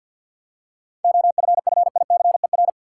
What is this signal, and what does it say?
Morse code, heard as a keyed tone, about 50 wpm, 700 Hz. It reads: OFFICER
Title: Seven Letter Words - 50wpm